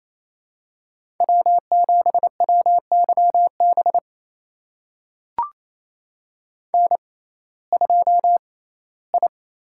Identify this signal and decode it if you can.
Morse code, heard as a keyed tone, about 28 wpm, spaced as for 17 wpm, 700 Hz. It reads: W7WYB N2I
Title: USA Call Signs - 5 Characters - 6x Character Spacing - 28wpm